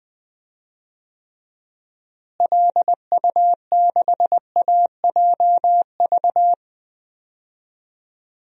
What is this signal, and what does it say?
Morse code, heard as a keyed tone, about 21 wpm, 700 Hz. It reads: LU6AJV
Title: DXCC-100 Call Signs - 6 Characters - Repeated 2x - 20wpm